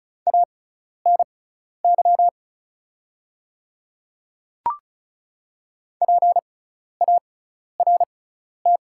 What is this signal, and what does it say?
Morse code, heard as a keyed tone, about 34 wpm, spaced as for 12 wpm, 700 Hz. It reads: ANY PART
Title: 6x Character Spacing - Top 200 Words - 35wpm